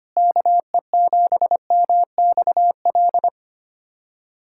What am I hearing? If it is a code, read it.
Morse code, heard as a keyed tone, about 25 wpm, 700 Hz. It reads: KE7MXL
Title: USA Callsign DE Callsign 25wpm